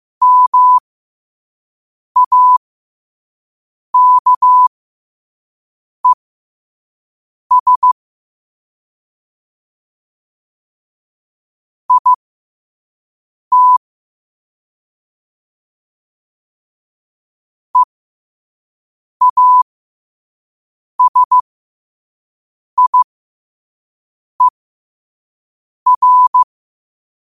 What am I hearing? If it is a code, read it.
Morse code, heard as a keyed tone, about 15 wpm, spaced as for 5 wpm, 1000 Hz. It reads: MAKES IT EASIER
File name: qotdfiles / 5wpm